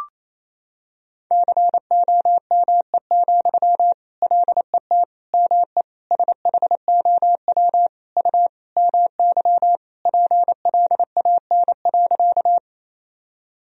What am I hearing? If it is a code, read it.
Morse code, heard as a keyed tone, about 28 wpm, 700 Hz. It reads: COME, LET ME SHOW U MY PLAN.
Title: Sentences from Top 500 Words and Abbreviations 28wpm